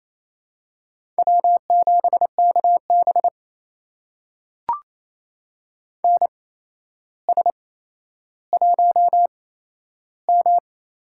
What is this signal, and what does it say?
Morse code, heard as a keyed tone, about 27 wpm, spaced as for 12 wpm, 700 Hz. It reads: W7KB NS1M
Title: USA Call Signs - 8x Character Spacing - 28wpm